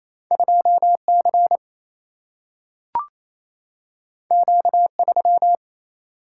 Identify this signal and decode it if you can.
Morse code, heard as a keyed tone, about 28 wpm, 700 Hz. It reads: 2C Q3
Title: Single-Letter Single-Number 28wpm